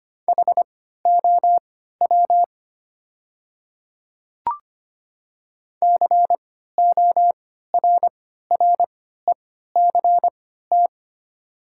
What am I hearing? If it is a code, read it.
Morse code, heard as a keyed tone, about 25 wpm, spaced as for 14 wpm, 700 Hz. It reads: HOW CORRECT
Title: 3x Character Spacing - Top 500 Words - 25wpm